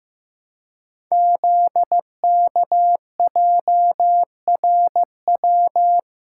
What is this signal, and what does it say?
Morse code, heard as a keyed tone, about 15 wpm, 700 Hz. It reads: ZKJRW
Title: Code Groups - 5 Letter:Numbers - 15wpm